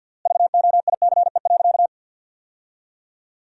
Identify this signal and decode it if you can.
Morse code, heard as a keyed tone, about 50 wpm, 700 Hz. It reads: VOICE.
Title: 3x Word-Spacing - Sentences from Top 500 Words - 50wpm